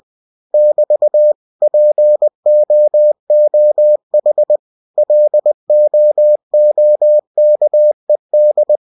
Morse code, heard as a keyed tone, about 20 wpm, 600 Hz. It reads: <BT> POOH LOOKED
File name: cw-books / winnie-the-pooh / winnie-20wpm-page-0052